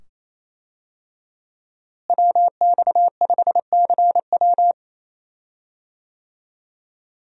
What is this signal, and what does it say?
Morse code, heard as a keyed tone, about 28 wpm, 700 Hz. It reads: WX5CW
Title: USA Call Signs - 5 Characters - Repeated 2x - 28wpm